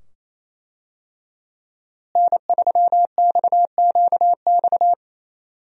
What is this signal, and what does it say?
Morse code, heard as a keyed tone, about 28 wpm, 700 Hz. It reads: N3XQX
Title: USA Call Signs - 5 Characters - Repeated 2x - 28wpm